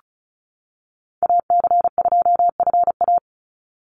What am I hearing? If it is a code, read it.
Morse code, heard as a keyed tone, about 34 wpm, 700 Hz. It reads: AC2FA
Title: Call Signs - 35wpm